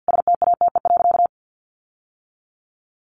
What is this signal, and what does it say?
Morse code, heard as a keyed tone, about 50 wpm, 700 Hz. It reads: STATE.
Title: Sentences from Top 1400 Words 50wpm